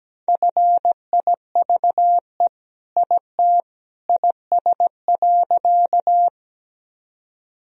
Morse code, heard as a keyed tone, about 17 wpm, 700 Hz. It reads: FIVE IT IS.
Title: Sentences from Top 1000 Words - Review Part 2 - 17wpm